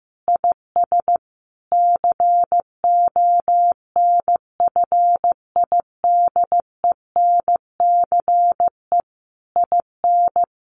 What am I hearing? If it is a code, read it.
Morse code, heard as a keyed tone, about 15 wpm, 700 Hz. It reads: IS CONFIDENCE IN
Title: Sentences from Top 1500 Words 15wpm